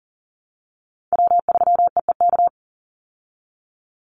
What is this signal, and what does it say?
Morse code, heard as a keyed tone, about 40 wpm, 700 Hz. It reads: W3EEK
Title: Call Signs Repeated 3x - 40wpm